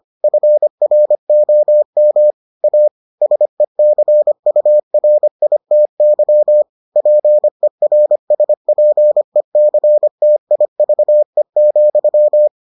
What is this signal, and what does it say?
Morse code, heard as a keyed tone, about 25 wpm, 600 Hz. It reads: FROM A SECURITY PERSPECTIVE,